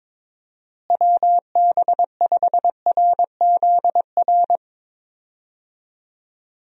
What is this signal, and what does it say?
Morse code, heard as a keyed tone, about 22 wpm, 700 Hz. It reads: WB5RZR
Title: USA Call Signs - 6 Characters - Repeated 3x - 22wpm